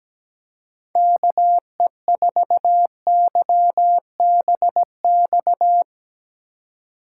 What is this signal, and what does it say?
Morse code, heard as a keyed tone, about 17 wpm, 700 Hz. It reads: KE4YBX
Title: USA Call Signs - 6 Characters - 4x Character Spacing - 17wpm